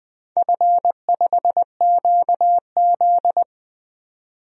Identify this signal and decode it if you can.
Morse code, heard as a keyed tone, about 20 wpm, 700 Hz. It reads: F5QZ
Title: DXCC-100 Call Signs - 4 Characters - 20wpm